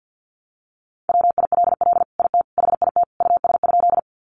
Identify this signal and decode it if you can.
Morse code, heard as a keyed tone, about 48 wpm, 700 Hz. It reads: WILL IT HIT US?
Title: Sentences from Top 1200 Words 50wpm